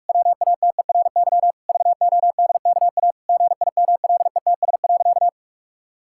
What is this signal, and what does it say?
Morse code, heard as a keyed tone, about 46 wpm, 700 Hz. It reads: WATERY VODKA GIMLETS.
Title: Pangram Challenge v2 - 45wpm